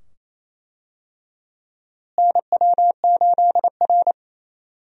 Morse code, heard as a keyed tone, about 27 wpm, 700 Hz. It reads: NW8R
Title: USA Call Signs - 4 Characters - Repeated 2x - 28wpm